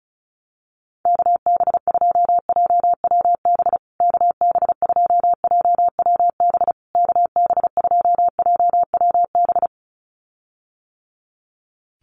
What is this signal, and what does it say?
Morse code, heard as a keyed tone, about 34 wpm, 700 Hz. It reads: KB2JWB KB2JWB KB2JWB
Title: USA Call Signs Repeated 3x 35wpm